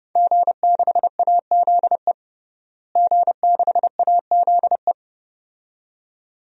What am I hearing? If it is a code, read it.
Morse code, heard as a keyed tone, about 31 wpm, 700 Hz. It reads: G6AZE G6AZE
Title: DXCC-100 Call Signs - 5 Characters - Repeated 2x - 30wpm